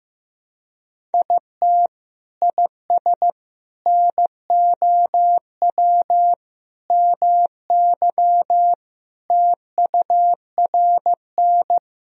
Morse code, heard as a keyed tone, about 15 wpm, 700 Hz. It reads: IT IS NOW MY TURN
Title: Sentences from Top 200 Words and Abbreviations 15wpm